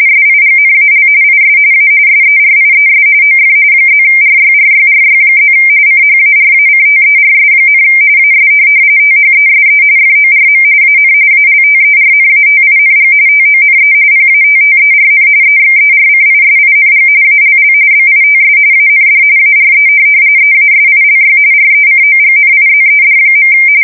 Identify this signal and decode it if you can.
RTTY, about 45.45 baud. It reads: SIGNAL IS A SERIES OF +RYRYRY+ CHARACTERS, AS THESE FORM AN ALTERNATING TONE PATTERN EXERCISING ALL BITS AND ARE EASILY RECOGNIZED.
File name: fsk170 / teletype-RTTY